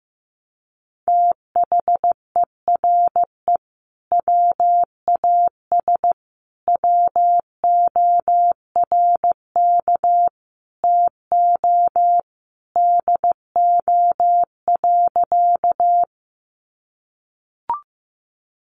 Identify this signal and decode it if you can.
Morse code, heard as a keyed tone, about 15 wpm, 700 Hz. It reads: THERE WAS WORK TO DO.
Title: Sentences from Top 200 Words 15wpm